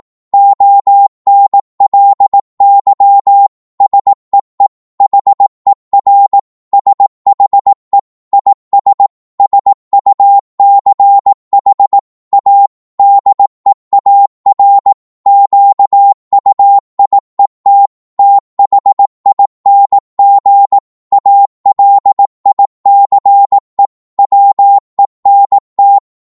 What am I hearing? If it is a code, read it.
Morse code, heard as a keyed tone, about 18 wpm, 800 Hz. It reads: ONLY SEE HER SHE IS SUCH A DEAR QUIET THING ALICE WENT
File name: Morse / psg-18wpm